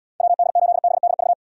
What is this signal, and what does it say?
Morse code, heard as a keyed tone, about 75 wpm, 700 Hz. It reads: WD8BDV